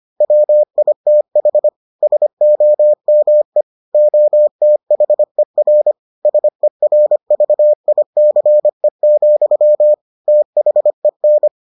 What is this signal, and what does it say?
Morse code, heard as a keyed tone, about 25 wpm, 600 Hz. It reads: WITH SOME OTHER SERVICE, THEN